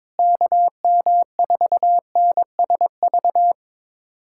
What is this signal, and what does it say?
Morse code, heard as a keyed tone, about 22 wpm, 700 Hz. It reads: KM4NSV
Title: USA Callsign DE Callsign - 6 Characters - 22wpm